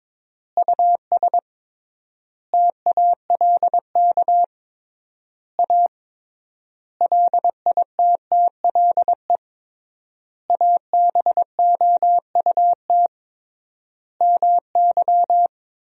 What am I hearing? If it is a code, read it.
Morse code, heard as a keyed tone, about 21 wpm, 700 Hz. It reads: US TALK A LITTLE ABOUT MY
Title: 3x Word-Spacing - Sentences from Top 400 Words - 22wpm